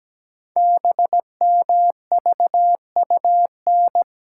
Morse code, heard as a keyed tone, about 17 wpm, 700 Hz. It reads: BMVUN
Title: Code Groups - 5 Letters - 17wpm